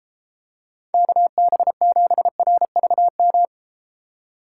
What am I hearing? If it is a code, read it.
Morse code, heard as a keyed tone, about 33 wpm, 700 Hz. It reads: KB7RVM